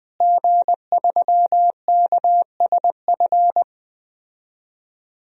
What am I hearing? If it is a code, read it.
Morse code, heard as a keyed tone, about 20 wpm, 700 Hz. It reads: G3KSF
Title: DXCC-100 Call Signs - 5 Characters - Repeated 3x - 20wpm